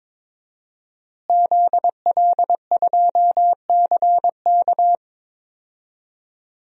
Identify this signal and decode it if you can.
Morse code, heard as a keyed tone, about 22 wpm, 700 Hz. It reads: ZL2CK
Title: DXCC-100 Call Signs - 5 Characters - Repeated 3x - 22wpm